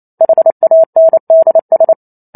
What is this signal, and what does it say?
Morse code, heard as a keyed tone, about 28 wpm, 650 Hz. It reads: HANDS
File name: Chapter 2 - 28 WPM